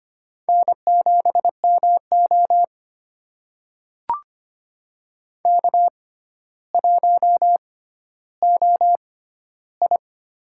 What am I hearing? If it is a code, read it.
Morse code, heard as a keyed tone, about 25 wpm, spaced as for 13 wpm, 700 Hz. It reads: N7MO K1OI
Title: USA Call Signs - 4 Characters - 6x Character Spacing - 25wpm